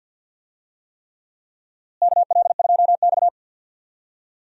Morse code, heard as a keyed tone, about 47 wpm, 700 Hz. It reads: KL2X
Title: USA Call Signs - 4 Characters - Repeated 3x - 50wpm